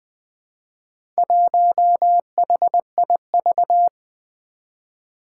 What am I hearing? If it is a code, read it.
Morse code, heard as a keyed tone, about 20 wpm, 700 Hz. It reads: 1HIV
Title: Four Challenging Letter-Number Combinations 20wpm